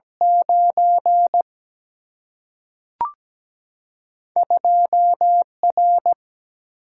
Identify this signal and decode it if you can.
Morse code, heard as a keyed tone, about 17 wpm, 700 Hz. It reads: 9 2 R
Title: Single-Letter Single-Number 17wpm